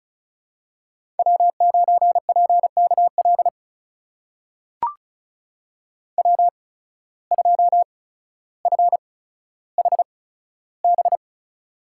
Morse code, heard as a keyed tone, about 34 wpm, spaced as for 15 wpm, 700 Hz. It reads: W9PKL W2FHB
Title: USA Call Signs - 5 Characters - 8x Character Spacing - 35wpm